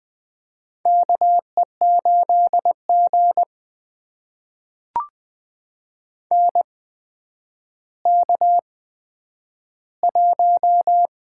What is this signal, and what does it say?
Morse code, heard as a keyed tone, about 20 wpm, spaced as for 10 wpm, 700 Hz. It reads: KE8G NK1
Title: USA Call Signs - 4 Characters - 8x Character Spacing - 20wpm